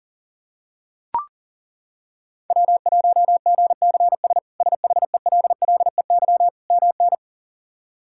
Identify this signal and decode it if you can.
Morse code, heard as a keyed tone, about 40 wpm, 700 Hz. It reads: W1GCS SHELLEY MN
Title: K1USN SST QSO Practice - 40wpm